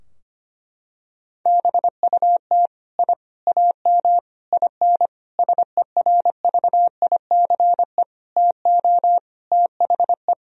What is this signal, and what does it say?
Morse code, heard as a keyed tone, about 25 wpm, 700 Hz. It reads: BUT I AM IN SERVICE TO THE
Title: Sentences from Top 1000 Words - Review Part 4 - 25wpm